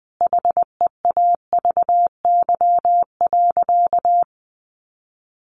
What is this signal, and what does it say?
Morse code, heard as a keyed tone, about 20 wpm, 700 Hz. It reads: HEAVY.
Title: Sentences from Top 1100 Words 20wpm